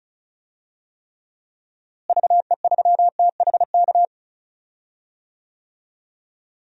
Morse code, heard as a keyed tone, about 36 wpm, 700 Hz. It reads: VE3THK
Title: DXCC-100 Call Signs - 6 Characters - Repeated 3x - 35wpm